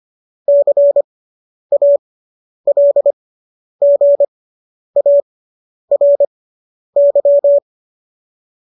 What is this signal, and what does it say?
Morse code, heard as a keyed tone, about 25 wpm, spaced as for 10 wpm, 570 Hz. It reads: CALGARY